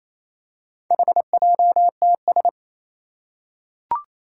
Morse code, heard as a keyed tone, about 29 wpm, 700 Hz. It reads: HJTS
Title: Four Challenging Letter-Number Combinations 28wpm